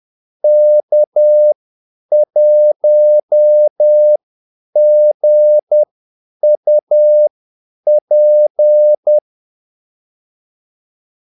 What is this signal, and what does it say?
Morse code, heard as a keyed tone, about 10 wpm, spaced as for 8 wpm, 600 Hz. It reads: K1GUP
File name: Callsigns-10-8wpm